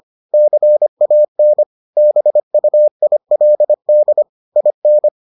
Morse code, heard as a keyed tone, about 25 wpm, 600 Hz. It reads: CAN BUILD IN